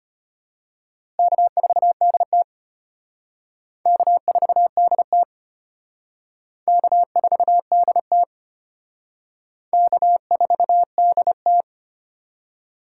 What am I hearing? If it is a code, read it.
Morse code, heard as a keyed tone, about 31 wpm, 700 Hz. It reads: K4DT K4DT K4DT K4DT
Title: Speed Racing - USA 1x2 Call Signs 25wpm